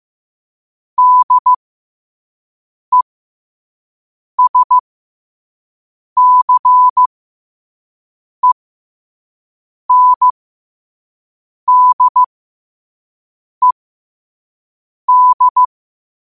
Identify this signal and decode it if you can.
Morse code, heard as a keyed tone, about 15 wpm, spaced as for 5 wpm, 1000 Hz. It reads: DESCENDED
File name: qotdfiles / 5wpm